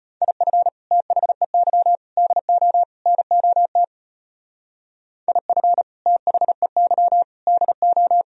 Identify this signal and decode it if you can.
Morse code, heard as a keyed tone, about 31 wpm, 700 Hz. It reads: IF THEY DO NOT IF THEY DO
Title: Speed-Racing - Sets of 4 Words 25wpm